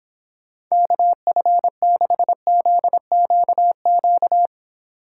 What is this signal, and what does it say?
Morse code, heard as a keyed tone, about 26 wpm, 700 Hz. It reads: KF6ZQQ